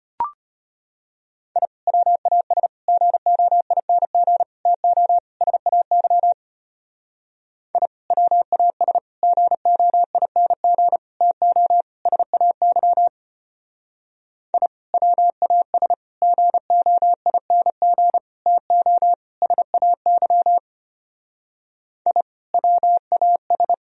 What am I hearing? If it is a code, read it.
Morse code, heard as a keyed tone, about 31 wpm, 700 Hz. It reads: I WAS GOING TO SAY I WAS GOING TO SAY I WAS GOING TO SAY I WAS
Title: Speed-Racing - Sets of 5 Words 25wpm